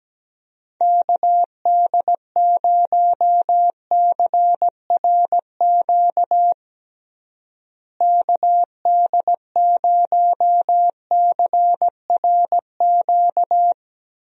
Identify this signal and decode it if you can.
Morse code, heard as a keyed tone, about 17 wpm, 700 Hz. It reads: KD0CRQ KD0CRQ
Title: USA Call Signs - 6 Characters - Repeated 3x - 17wpm